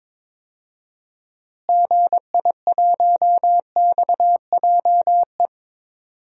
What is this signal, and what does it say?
Morse code, heard as a keyed tone, about 22 wpm, 700 Hz. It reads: GI1XJE
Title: DXCC-100 Call Signs - 6 Characters - Repeated 2x - 22wpm